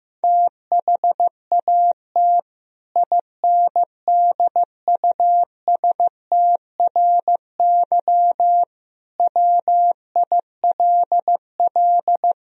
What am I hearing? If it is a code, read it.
Morse code, heard as a keyed tone, about 15 wpm, 700 Hz. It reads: THAT INDUSTRY WILL
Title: Sentences from Top 1000 Words - Review Part 4 - 15wpm